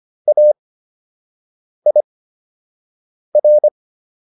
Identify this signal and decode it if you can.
Morse code, heard as a keyed tone, about 25 wpm, spaced as for 6 wpm, 600 Hz. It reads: AIR